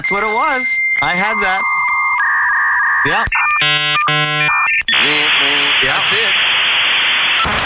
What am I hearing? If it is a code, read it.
Dial-up modem handshake, with speech.